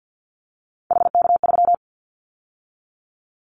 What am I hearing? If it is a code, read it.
Morse code, heard as a keyed tone, about 50 wpm, 700 Hz. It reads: HK3
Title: Callsign Prefixes for Top 100 DXCC - 50wpm